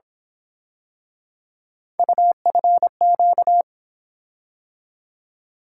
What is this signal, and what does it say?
Morse code, heard as a keyed tone, about 27 wpm, 700 Hz. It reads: UFQ
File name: Speed-Racing - Three Letter-Number Combinations - 22wpm